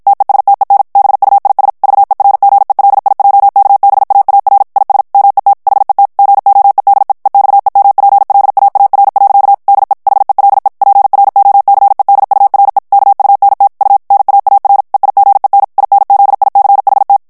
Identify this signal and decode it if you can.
Morse code, heard as a keyed tone, about 53 wpm, 800 Hz. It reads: TESTEN BUIS VERGELIJKBAAR IS MET HET GOEDE EXEMPLAAR. DE HELE PROCEDURE KUNT U NAAR EIGEN INZICHT
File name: HOLLANDAIS texte clair 2 vitesse 50 mots minute